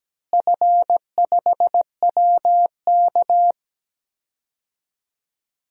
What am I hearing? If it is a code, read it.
Morse code, heard as a keyed tone, about 17 wpm, 700 Hz. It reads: F5WK
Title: DXCC-100 Call Signs - 4 Characters - Repeated 2x - 17wpm